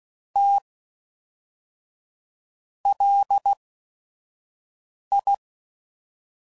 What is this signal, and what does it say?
Morse code, heard as a keyed tone, about 16 wpm, spaced as for 5 wpm, 800 Hz. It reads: T LI
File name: CW Code Groups @5WPM / QSOs / QSO WD0FBT